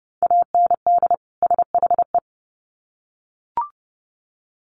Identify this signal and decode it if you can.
Morse code, heard as a keyed tone, about 30 wpm, 700 Hz. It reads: AND SHE
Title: Sets of 2 Words 30wpm